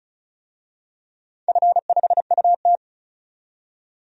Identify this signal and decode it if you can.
Morse code, heard as a keyed tone, about 36 wpm, 690 Hz. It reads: F5UT
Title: DXCC-100 Call Signs - 4 Characters - 35wpm